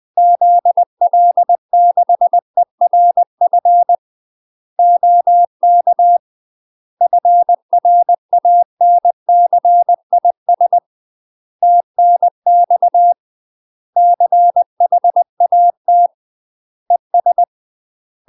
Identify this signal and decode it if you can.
Morse code, heard as a keyed tone, about 20 wpm, 700 Hz. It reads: ZL6ERF OK FRANCIS TNX CHAT ES
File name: UK MASTER QSOs TRAINING/20WPM